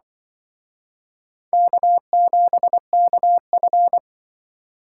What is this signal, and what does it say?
Morse code, heard as a keyed tone, about 24 wpm, 690 Hz. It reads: K7KF
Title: Speed Racing - USA 1x2 Call Signs 20wpm